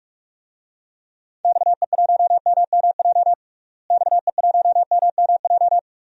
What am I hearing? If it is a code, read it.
Morse code, heard as a keyed tone, about 46 wpm, 700 Hz. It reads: XE1MMJ XE1MMJ
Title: DXCC-100 Call Signs - 6 Characters - Repeated 2x - 45wpm